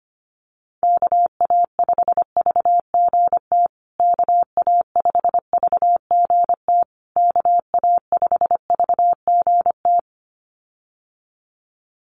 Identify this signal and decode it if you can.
Morse code, heard as a keyed tone, about 25 wpm, 700 Hz. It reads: KA5VGT KA5VGT KA5VGT
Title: USA Call Signs Repeated 3x 25wpm